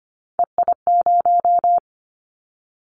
Morse code, heard as a keyed tone, about 25 wpm, 700 Hz. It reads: EI0
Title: Callsign Prefixes for Top 50 DXCC - 25wpm